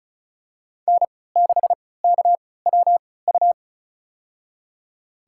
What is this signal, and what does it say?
Morse code, heard as a keyed tone, about 34 wpm, spaced as for 20 wpm, 700 Hz. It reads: N6KWU
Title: USA Call Signs - 5 Characters - 3x Character Spacing - 35wpm